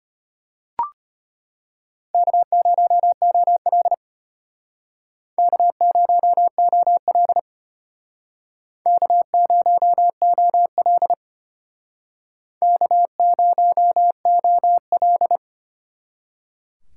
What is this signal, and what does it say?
Morse code, heard as a keyed tone, about 33 wpm, 700 Hz. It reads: K0OL K0OL K0OL K0OL